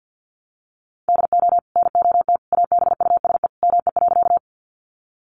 Morse code, heard as a keyed tone, about 48 wpm, 700 Hz. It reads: DO NOT ABUSE ME.
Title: Sentences from Top 1300 Words 50wpm